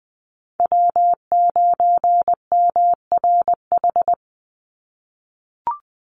Morse code, heard as a keyed tone, about 20 wpm, 690 Hz. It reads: W9MRH
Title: NA SOTA Chaser Callsigns 20wpm